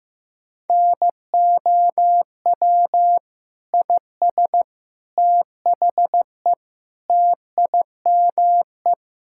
Morse code, heard as a keyed tone, about 15 wpm, 700 Hz. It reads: NOW IS THE TIME
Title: Pangram Challenge v2 - 15wpm